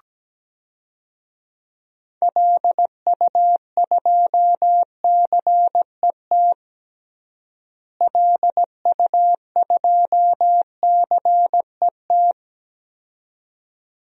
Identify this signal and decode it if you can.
Morse code, heard as a keyed tone, about 17 wpm, 700 Hz. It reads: LU2CET LU2CET
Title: DXCC-100 Call Signs - 6 Characters - Repeated 2x - 17wpm